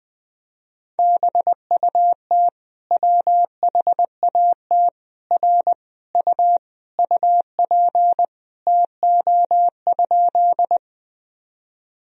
Morse code, heard as a keyed tone, about 20 wpm, 700 Hz. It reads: BUT WHAT R U UP TO?
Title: Sentences from Top 100 Words and Abbreviations 20wpm